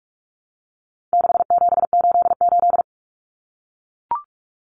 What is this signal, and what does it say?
Morse code, heard as a keyed tone, about 44 wpm, 700 Hz. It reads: 6788
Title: Numbers - 4-Digit 45wpm